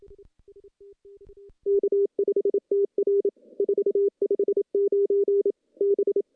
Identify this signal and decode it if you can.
Morse code, heard as a keyed tone, about 27 wpm, 410 Hz. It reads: K5TR 459 B
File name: CW-SS-k5tr